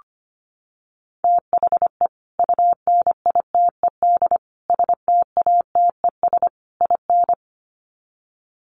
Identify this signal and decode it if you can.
Morse code, heard as a keyed tone, about 25 wpm, 700 Hz. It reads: THE UNITED STATES IN
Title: Sets of 4 Words 25wpm